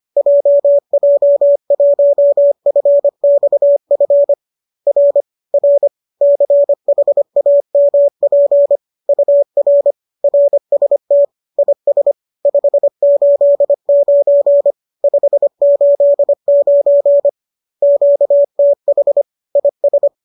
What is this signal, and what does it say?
Morse code, heard as a keyed tone, about 25 wpm, 570 Hz. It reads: JJ1FXF R R CHAMP UR RST IS 589 589 QTH IS